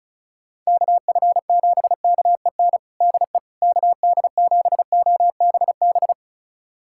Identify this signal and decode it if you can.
Morse code, heard as a keyed tone, about 36 wpm, 690 Hz. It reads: KF7KEN DE KD7OBB
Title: USA Callsign DE Callsign - 6 Characters - 35wpm